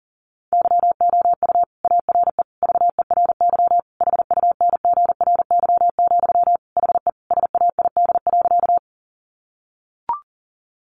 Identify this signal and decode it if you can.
Morse code, heard as a keyed tone, about 40 wpm, 700 Hz. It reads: YOU ARE VERY HUNGRY, HE SAID.
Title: Sentences from Top 1400 Words 40wpm